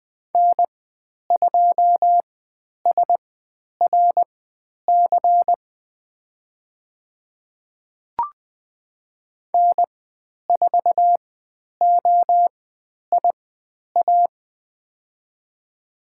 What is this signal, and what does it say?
Morse code, heard as a keyed tone, about 20 wpm, spaced as for 10 wpm, 700 Hz. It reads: N2SRC N4OIA
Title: 102 Lesson 17g - Letter u - Call Signs 20wpm